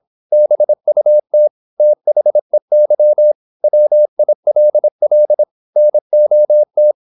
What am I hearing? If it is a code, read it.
Morse code, heard as a keyed tone, about 26 wpm, 600 Hz. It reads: BUT THEY WILL NOT
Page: Never 26wpm | CW med Gnister